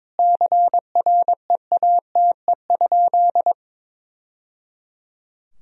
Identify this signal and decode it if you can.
Morse code, heard as a keyed tone, about 22 wpm, 700 Hz. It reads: CREATE?
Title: Sentences from Top 1000 Words - Review Part 3 - 22wpm